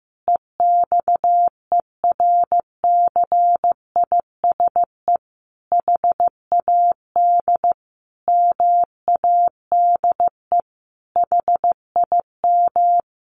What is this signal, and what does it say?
Morse code, heard as a keyed tone, about 15 wpm, 700 Hz. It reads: EXERCISE HAD MADE HIM
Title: Sentences from Top 1400 Words 15wpm